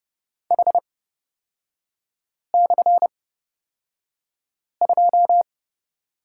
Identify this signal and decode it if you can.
Morse code, heard as a keyed tone, about 30 wpm, spaced as for 5 wpm, 700 Hz. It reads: H/2
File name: Lesson 37e - Letter x - Three Characters 30wpm